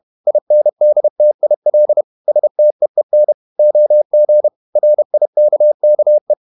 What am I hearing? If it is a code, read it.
Morse code, heard as a keyed tone, about 31 wpm, 600 Hz. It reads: INDTIL STEEN OG RIKKE
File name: Ferie QRQ 31wpm